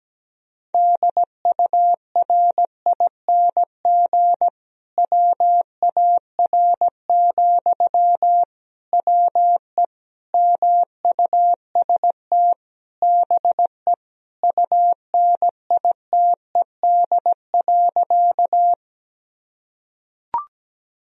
Morse code, heard as a keyed tone, about 17 wpm, 700 Hz. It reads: DURING WAR, WE MUST BE UNITED.
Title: Sentences from Top 1000 Words - Review Part 4 - 17wpm